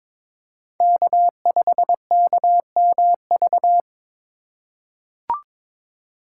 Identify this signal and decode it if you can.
Morse code, heard as a keyed tone, about 22 wpm, 700 Hz. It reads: K5KMV
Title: USA Call Signs - 5 Characters - 4x Character Spacing - 22wpm